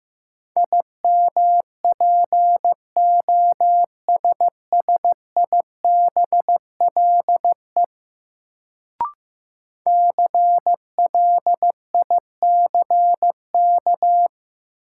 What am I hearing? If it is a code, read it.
Morse code, heard as a keyed tone, about 15 wpm, 700 Hz. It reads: IMPOSSIBLE CLICK
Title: Sentences from Top 1100 Words - New Word Drill - 15wpm